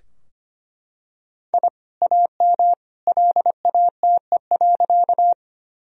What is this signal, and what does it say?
Morse code, heard as a keyed tone, about 25 wpm, 700 Hz. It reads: I AM LATE.
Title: Sentences from Top 1000 Words - Review Part 2 - 25wpm